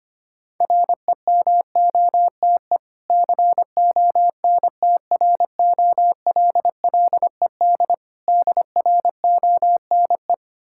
Morse code, heard as a keyed tone, about 25 wpm, 700 Hz. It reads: REMOTE CONTROLLED DRONE
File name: practice / 25wpm / Difficult Words with Hints - 25wpm